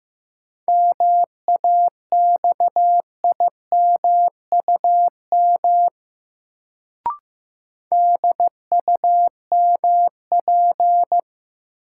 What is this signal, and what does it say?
Morse code, heard as a keyed tone, about 15 wpm, 700 Hz. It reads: MAXIMUM DUMP
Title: Sentences from Top 1300 Words - New Word Drill Alt - 15wpm